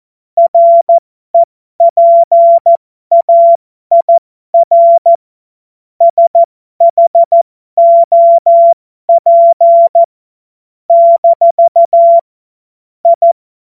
Morse code, heard as a keyed tone, about 14 wpm, spaced as for 12 wpm, 680 Hz. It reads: REPAIR SHOP - I
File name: Morsekurs / Witze